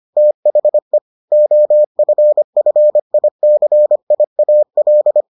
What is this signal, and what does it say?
Morse code, heard as a keyed tone, about 25 wpm, 600 Hz. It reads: THE OFFICIAL